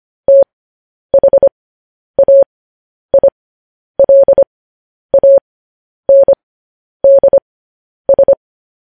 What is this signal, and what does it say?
Morse code, heard as a keyed tone, about 25 wpm, spaced as for 10 wpm, 550 Hz. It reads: THAILANDS
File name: podcast / News headlines-1x-25x10